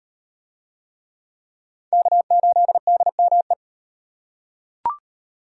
Morse code, heard as a keyed tone, about 36 wpm, 700 Hz. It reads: K8DME